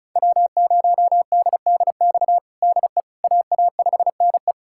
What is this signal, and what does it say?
Morse code, heard as a keyed tone, about 36 wpm, 700 Hz. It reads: W0DDX DE AA5NE
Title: USA Callsign DE Callsign - 5 Characters - 35wpm